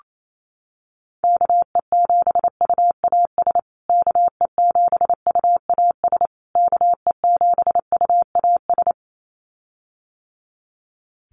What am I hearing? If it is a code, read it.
Morse code, heard as a keyed tone, about 28 wpm, 700 Hz. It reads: KE7UAS KE7UAS KE7UAS
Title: USA Call Signs Repeated 3x 28wpm